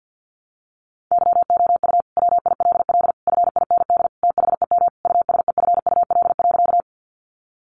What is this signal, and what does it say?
Morse code, heard as a keyed tone, about 50 wpm, 700 Hz. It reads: YOU WILL FIND THEM USEFUL.
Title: Sentences from Top 700 Words 50wpm